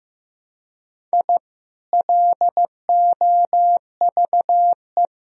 Morse code, heard as a keyed tone, about 15 wpm, 700 Hz. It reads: I LOVE
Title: Sentences from Top 1000 Words - Review Part 1 - 15wpm